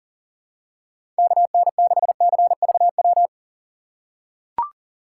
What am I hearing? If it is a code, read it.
Morse code, heard as a keyed tone, about 40 wpm, 700 Hz. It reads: KN6CVW
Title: USA Call Signs - 6 Characters - 4x Character Spacing - 40wpm